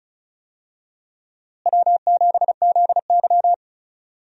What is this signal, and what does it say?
Morse code, heard as a keyed tone, about 36 wpm, 700 Hz. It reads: W7ZY
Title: USA Call Signs - 4 Characters - Repeated 3x - 35wpm